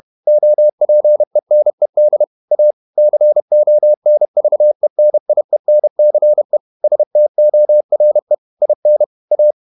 Morse code, heard as a keyed tone, about 31 wpm, 600 Hz. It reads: OPENED A CONVENIENCE STORE IN A
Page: Never 31wpm | CW med Gnister